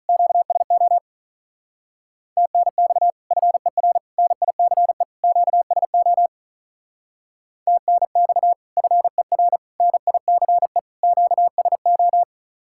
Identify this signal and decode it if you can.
Morse code, heard as a keyed tone, about 43 wpm, 700 Hz. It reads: QSO TNX FER NICE QSO TNX FER NICE QSO
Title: Speed-Racing Common QSO Phrases Encore - 35wpm